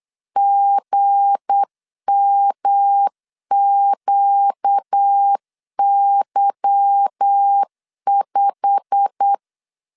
Morse code, heard as a keyed tone, about 8 wpm, 800 Hz. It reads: GMQY5
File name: LEZIONI / 8 YJ45 / 8 YJ45 8wpm 3214